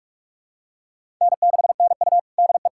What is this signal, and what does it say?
Morse code, heard as a keyed tone, about 41 wpm, 700 Hz. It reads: N6NU DE
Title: USA Callsign DE Callsign - 4 Characters - 45wpm